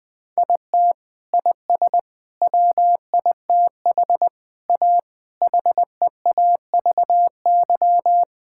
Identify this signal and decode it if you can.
Morse code, heard as a keyed tone, about 20 wpm, 700 Hz. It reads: IT IS WITH A HEAVY
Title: Sets of 8 Words - 20wpm